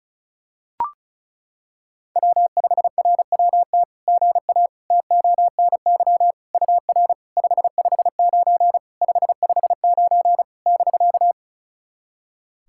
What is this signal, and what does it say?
Morse code, heard as a keyed tone, about 36 wpm, 700 Hz. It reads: W5RWT GA TONY UR 559 559 <BK>
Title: SOTA QSO Practice 35wpm